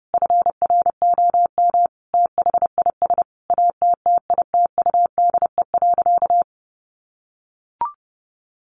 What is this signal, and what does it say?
Morse code, heard as a keyed tone, about 28 wpm, 700 Hz. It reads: FROM THIS ATTITUDE.
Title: Sentences from Top 1300 Words 30wpm